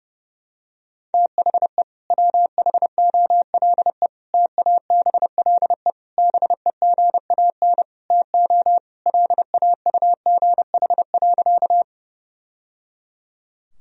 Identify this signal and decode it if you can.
Morse code, heard as a keyed tone, about 32 wpm, 700 Hz. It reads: THE WHOLE TABLE BEGAN TO LAUGH.
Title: Sentences from Top 1000 Words - Review Part 2 - 30wpm